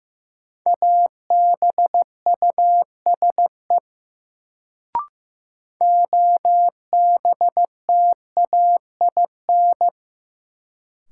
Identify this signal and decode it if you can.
Morse code, heard as a keyed tone, about 15 wpm, 700 Hz. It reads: ABUSE OBTAIN
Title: Sentences from Top 1300 Words - New Word Drill Alt - 15wpm